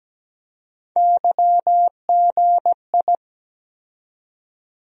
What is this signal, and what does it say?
Morse code, heard as a keyed tone, about 17 wpm, 700 Hz. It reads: YGI
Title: USA Call Sign Postfixes - 3 Letters 17wpm